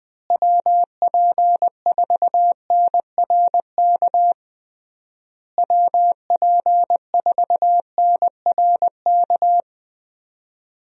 Morse code, heard as a keyed tone, about 20 wpm, 700 Hz. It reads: WP4NRK WP4NRK
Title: USA Call Signs - 6 Characters - Repeated 3x - 20wpm